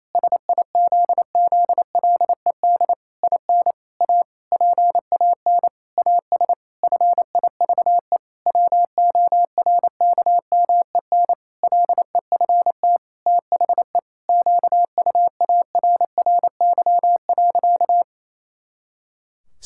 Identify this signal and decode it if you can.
Morse code, heard as a keyed tone, about 28 wpm, 700 Hz. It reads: SIZZLED IN A PAN AS FIVE WORKMEN LEFT THE QUARRY.
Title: Pangram Challenge v2 - 28wpm